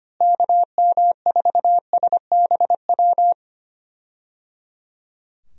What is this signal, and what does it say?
Morse code, heard as a keyed tone, about 25 wpm, 700 Hz. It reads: KM4SBW
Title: USA Call Signs - 6 Characters - Repeated 3x - 25wpm